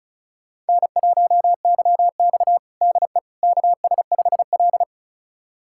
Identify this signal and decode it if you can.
Morse code, heard as a keyed tone, about 36 wpm, 700 Hz. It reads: N1YX DE KS5L
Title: USA Callsign DE Callsign - 4 Characters - 35wpm